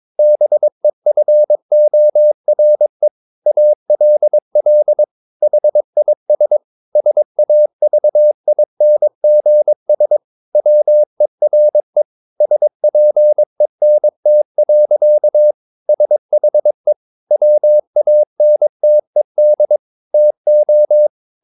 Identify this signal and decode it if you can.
Morse code, heard as a keyed tone, about 22 wpm, 600 Hz. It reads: BEFORE ALL HIS SAVINGS WERE SPENT. SHE WANTED TO
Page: Romanen Never af Ken Follet på engelsk.